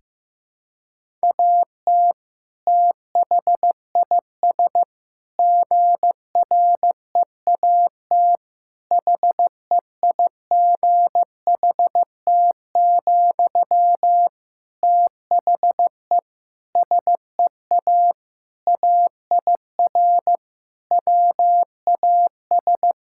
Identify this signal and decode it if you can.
Morse code, heard as a keyed tone, about 16 wpm, 700 Hz. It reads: AT THIS GREAT HEIGHT, THE SEA AIR WAS
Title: Sentences from Top 1600 Words 15wpm